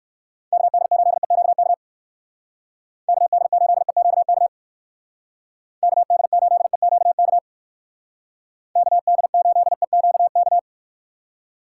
Morse code, heard as a keyed tone, about 64 wpm, 700 Hz. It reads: KD8EQK KD8EQK KD8EQK KD8EQK